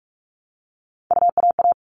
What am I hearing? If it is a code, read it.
Morse code, heard as a keyed tone, about 44 wpm, 700 Hz. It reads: UAA
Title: All Cut Numbers - 3-Digit 45wpm